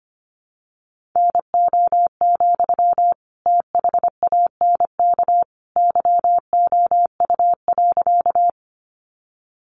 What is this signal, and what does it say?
Morse code, heard as a keyed tone, about 25 wpm, 700 Hz. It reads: NO, THANK YOU.
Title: Sentences from Top 600 Words 25wpm